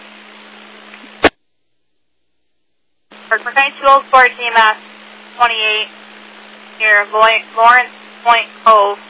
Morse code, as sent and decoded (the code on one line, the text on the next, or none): none